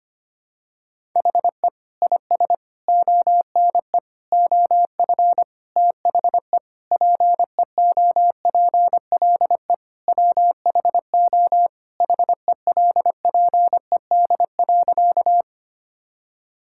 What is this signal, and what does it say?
Morse code, heard as a keyed tone, about 25 wpm, 700 Hz. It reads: HE IS ONE OF THE PEOPLE WHO HELPED.
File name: practice / 25wpm / Sentences with 4-Word Phrases - 25wpm